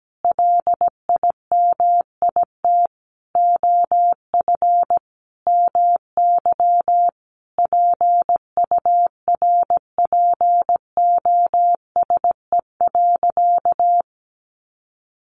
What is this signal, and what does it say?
Morse code, heard as a keyed tone, about 17 wpm, 700 Hz. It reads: LIMIT OF MY PURPOSE.
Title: Sentences from Top 1500 Words 17wpm